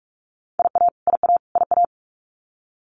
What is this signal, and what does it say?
Morse code, heard as a keyed tone, about 44 wpm, 700 Hz. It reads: IA IA IA
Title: US States Repeated 3x 45wpm